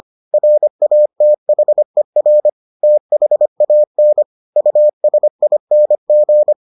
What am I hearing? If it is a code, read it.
Morse code, heard as a keyed tone, about 25 wpm, 600 Hz. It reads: RATHER THAN USING